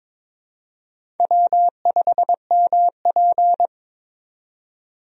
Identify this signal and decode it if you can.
Morse code, heard as a keyed tone, about 23 wpm, 700 Hz. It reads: W5MP
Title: USA Call Signs - 4 Characters - 22wpm